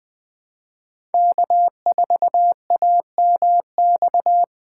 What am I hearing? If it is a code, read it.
Morse code, heard as a keyed tone, about 20 wpm, 700 Hz. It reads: K4AMX
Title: USA Call Signs - 5 Characters - 3x Character Spacing - 20wpm